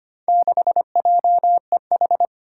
Morse code, heard as a keyed tone, about 25 wpm, 700 Hz. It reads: 6JEH
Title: Four Challenging Letter-Number Combinations 25wpm